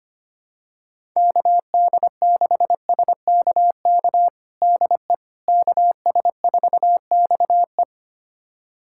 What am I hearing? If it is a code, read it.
Morse code, heard as a keyed tone, about 25 wpm, 700 Hz. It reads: KD6SKK DE KS4XE
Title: USA Callsign DE Callsign 25wpm